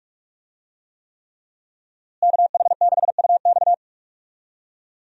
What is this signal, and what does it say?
Morse code, heard as a keyed tone, about 47 wpm, 700 Hz. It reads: KH6UX
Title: USA Call Signs - 5 Characters - Repeated 3x - 45wpm